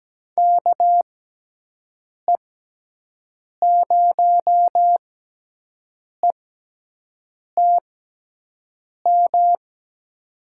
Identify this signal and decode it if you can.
Morse code, heard as a keyed tone, about 17 wpm, spaced as for 6 wpm, 700 Hz. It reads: KE0ETM
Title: USA Call Signs - 6 Characters - 6x Character Spacing - 17wpm